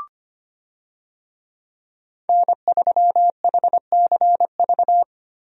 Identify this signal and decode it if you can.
Morse code, heard as a keyed tone, about 25 wpm, 700 Hz. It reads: N3HCV
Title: USA Call Signs - 5 Characters - 25wpm